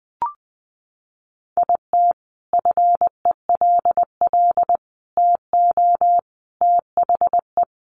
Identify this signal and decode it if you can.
Morse code, heard as a keyed tone, about 20 wpm, 700 Hz. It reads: IT FELL TO THE
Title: A Sound Of Thunder 20wpm